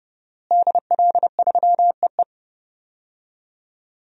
Morse code, heard as a keyed tone, about 30 wpm, 700 Hz. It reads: DL3EE
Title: DXCC-100 Call Signs - 5 Characters - Repeated 3x - 30wpm